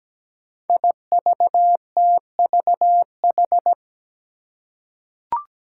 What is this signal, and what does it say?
Morse code, heard as a keyed tone, about 17 wpm, 700 Hz. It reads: IVTVH
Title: Five Challenging Letter-Number Combinations 17wpm